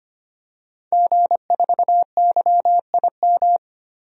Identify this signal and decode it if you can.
Morse code, heard as a keyed tone, about 25 wpm, 700 Hz. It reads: G4YIM
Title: DXCC-100 Call Signs - 5 Characters - Repeated 3x - 25wpm